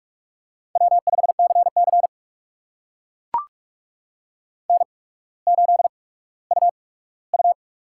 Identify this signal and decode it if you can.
Morse code, heard as a keyed tone, about 44 wpm, spaced as for 19 wpm, 700 Hz. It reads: W5CC N8UU
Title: USA Call Signs - 8x Character Spacing - 45wpm